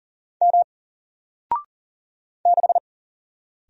Morse code, heard as a keyed tone, about 40 wpm, 700 Hz. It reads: M 6
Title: Single Character (Letter:Number:Punctuation:Prosign) Alt 40wpm